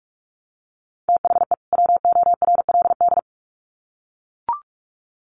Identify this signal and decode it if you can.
Morse code, heard as a keyed tone, about 44 wpm, 700 Hz. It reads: THE WORLD
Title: Sets of 2 Words 45wpm